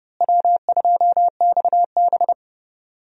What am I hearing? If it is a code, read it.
Morse code, heard as a keyed tone, about 30 wpm, 700 Hz. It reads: W2XB
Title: USA Call Signs - 4 Characters - Repeated 3x - 30wpm